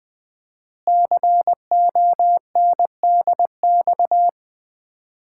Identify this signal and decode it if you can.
Morse code, heard as a keyed tone, about 20 wpm, 700 Hz. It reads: CONDX
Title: Common QSO Phrases - 4x Word-Spacing - 20wpm